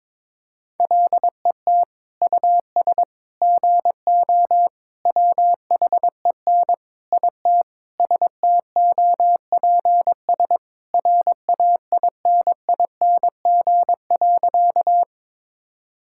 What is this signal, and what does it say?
Morse code, heard as a keyed tone, about 22 wpm, 700 Hz. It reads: LET US GO WHEN IT STOPS RAINING.
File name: practice / 22wpm / Sentences with 2-Word Phrases - 22wpm